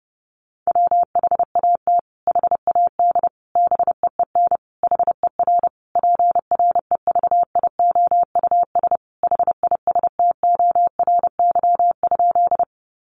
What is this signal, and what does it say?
Morse code, heard as a keyed tone, about 30 wpm, 700 Hz. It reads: WHAT HAD BEEN HER PREVIOUS HISTORY?
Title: Sentences from Top 1500 Words 30wpm